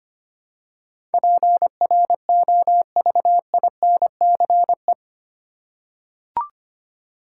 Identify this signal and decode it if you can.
Morse code, heard as a keyed tone, about 25 wpm, 700 Hz. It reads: PROVINCE
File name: Eight Letter Words - 25wpm